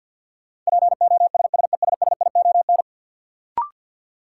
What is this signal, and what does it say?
Morse code, heard as a keyed tone, about 53 wpm, 700 Hz. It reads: POSSESSION
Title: Sentences from Top 1400 Words - New Word Drill Alt - 50wpm